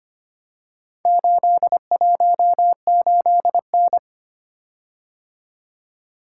Morse code, heard as a keyed tone, about 25 wpm, 700 Hz. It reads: 818N
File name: practice / 25wpm / Four Letter-Number Combinations - 25wpm